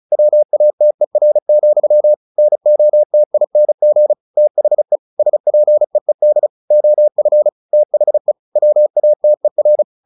Morse code, heard as a keyed tone, about 34 wpm, 600 Hz. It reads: WATER, NOTING THE SPEED OF THE WATER
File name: the old man and the sea 2 35wpm